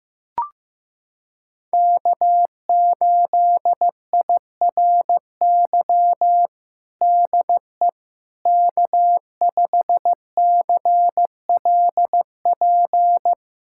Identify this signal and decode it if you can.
Morse code, heard as a keyed tone, about 15 wpm, 700 Hz. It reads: K8IRY DE K5CLP
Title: USA Callsign DE Callsign - 5 Characters - 15wpm